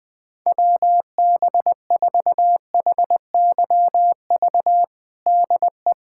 Morse code, heard as a keyed tone, about 21 wpm, 700 Hz. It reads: WB4HYV DE
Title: USA Callsign DE Callsign - 6 Characters - 20wpm